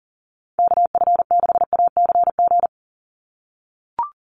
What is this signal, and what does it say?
Morse code, heard as a keyed tone, about 38 wpm, 700 Hz. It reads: KF6ACG
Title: USA Call Signs 40wpm